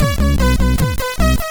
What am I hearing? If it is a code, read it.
There is no code.